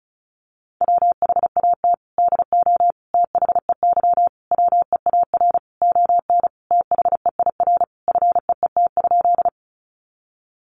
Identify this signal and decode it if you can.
Morse code, heard as a keyed tone, about 34 wpm, 700 Hz. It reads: WHAT DO THEY WEAR ON THEIR FEET?
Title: Sentences from Top 1500 Words 35wpm